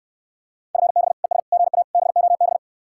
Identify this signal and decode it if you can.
Morse code, heard as a keyed tone, about 69 wpm, 700 Hz. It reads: FB ES GA BOB